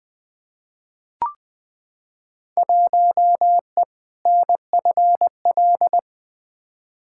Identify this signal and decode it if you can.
Morse code, heard as a keyed tone, about 20 wpm, 700 Hz. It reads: 1E NFL
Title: Field Day Exchanges - 20wpm